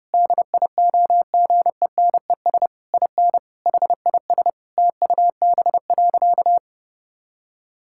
Morse code, heard as a keyed tone, about 32 wpm, 700 Hz. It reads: DIOGENES IN HIS TUB.
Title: Pangram Challenge v2 - 30wpm